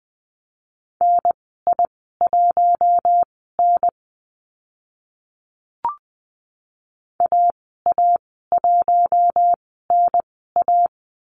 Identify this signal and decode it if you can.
Morse code, heard as a keyed tone, about 20 wpm, spaced as for 14 wpm, 700 Hz. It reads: NI1N AA1NA
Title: 042 Lesson 8g - Number 1 - Call Signs 20wpm